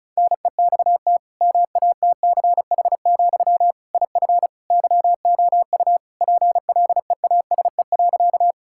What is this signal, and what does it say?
Morse code, heard as a keyed tone, about 36 wpm, 700 Hz. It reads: NEXT MATCH, IF YOU PLEASE.
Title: Sentences from Top 1600 Words 35wpm